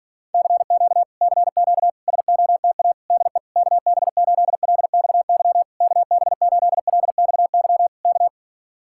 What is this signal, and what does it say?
Morse code, heard as a keyed tone, about 52 wpm, 700 Hz. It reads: CQ CQ SOTA DE KB8LXY KB8LXY K